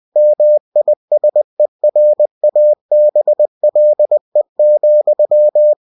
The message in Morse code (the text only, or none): MISERABLE,